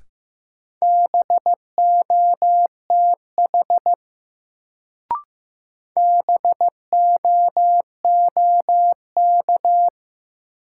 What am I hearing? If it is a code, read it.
Morse code, heard as a keyed tone, about 15 wpm, 700 Hz. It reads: BOTH BOOK
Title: Sentences from Top 300 Words - New Word Drill - 15wpm